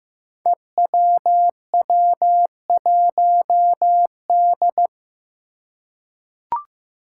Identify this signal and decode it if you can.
Morse code, heard as a keyed tone, about 15 wpm, 700 Hz. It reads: EWW1D
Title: Five Challenging Letter-Number Combinations 15wpm